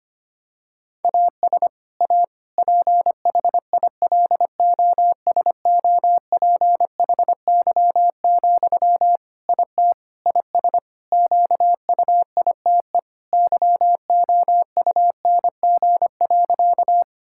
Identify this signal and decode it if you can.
Morse code, heard as a keyed tone, about 25 wpm, 700 Hz. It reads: AS A PHILOSOPHY, IT IS QUITE YOUNG.
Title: Sentences from Top 1000 Words - Review Part 6 - 25wpm